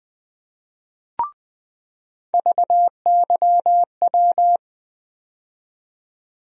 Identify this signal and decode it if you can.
Morse code, heard as a keyed tone, about 20 wpm, 700 Hz. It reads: VYW
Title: USA Call Sign Postfixes - 3 Letters 20wpm